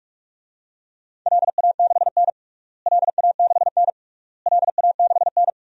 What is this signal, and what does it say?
Morse code, heard as a keyed tone, about 46 wpm, 700 Hz. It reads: LA6N LA6N LA6N
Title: DXCC-100 Call Signs - 4 Characters - Repeated 3x - 45wpm